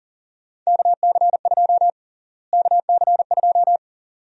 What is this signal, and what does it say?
Morse code, heard as a keyed tone, about 40 wpm, 700 Hz. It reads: KC2 KC2
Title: USA Call Sign Prefixes - Repeated 2x - 40wpm